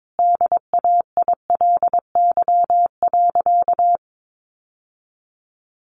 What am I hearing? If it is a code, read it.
Morse code, heard as a keyed tone, about 22 wpm, 700 Hz. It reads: DAILY.
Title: Sentences from Top 1100 Words 22wpm